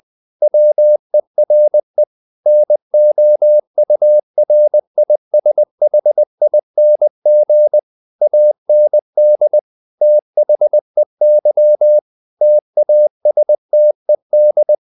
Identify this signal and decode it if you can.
Morse code, heard as a keyed tone, about 20 wpm, 600 Hz. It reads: WERE NOURISHING AND THEY TASTED
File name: the old man and the sea part 2 20wpm